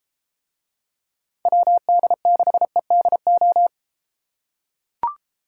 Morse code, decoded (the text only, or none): WD6EDO